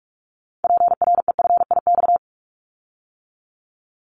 Morse code, heard as a keyed tone, about 46 wpm, 690 Hz. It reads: PREFIX
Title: Callsign Prefixes for Top 50 DXCC - 45wpm